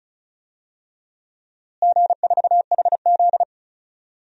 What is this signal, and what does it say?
Morse code, heard as a keyed tone, about 34 wpm, 700 Hz. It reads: G4HZ
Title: DXCC-100 Call Signs - 4 Characters - Repeated 3x - 35wpm